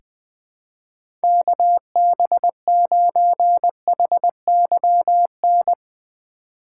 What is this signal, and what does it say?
Morse code, heard as a keyed tone, about 21 wpm, 700 Hz. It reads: KB9HYN
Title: USA Call Signs - 6 Characters - 4x Character Spacing - 20wpm